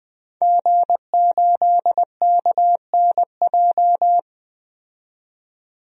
Morse code, heard as a keyed tone, about 20 wpm, 700 Hz. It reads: G8KNJ
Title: DXCC-100 Call Signs - 5 Characters - 20wpm